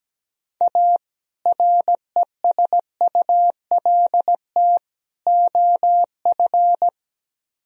Morse code, heard as a keyed tone, about 17 wpm, 700 Hz. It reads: A RESULT OF
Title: Speed-Racing - Sets of 3 Words 17wpm